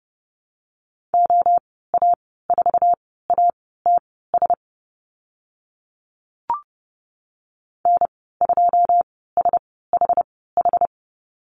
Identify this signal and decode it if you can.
Morse code, heard as a keyed tone, about 30 wpm, spaced as for 17 wpm, 700 Hz. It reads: OA4ATS N2SHH
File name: Lesson 15g - Number 5 - Call Signs 30wpm